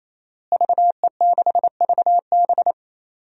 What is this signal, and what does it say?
Morse code, heard as a keyed tone, about 28 wpm, 700 Hz. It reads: VE6VB
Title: DXCC-100 Call Signs - 5 Characters - Repeated 3x - 28wpm